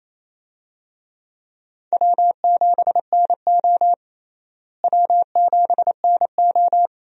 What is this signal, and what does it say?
Morse code, heard as a keyed tone, about 28 wpm, 700 Hz. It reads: W7NO W7NO
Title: USA Call Signs - 4 Characters - Repeated 2x - 28wpm